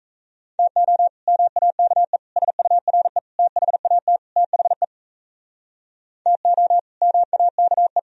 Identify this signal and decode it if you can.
Morse code, heard as a keyed tone, about 42 wpm, 700 Hz. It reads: TO MAKE SURE THAT THE TO MAKE